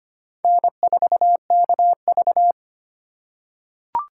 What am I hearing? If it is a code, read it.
Morse code, heard as a keyed tone, about 25 wpm, 700 Hz. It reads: N4KV
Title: USA Call Signs - 4x Character Spacing - 25wpm